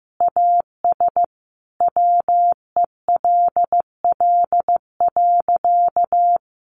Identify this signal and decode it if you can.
Morse code, heard as a keyed tone, about 15 wpm, 700 Hz. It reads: AS WELL.
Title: Sentences from Top 1300 Words 15wpm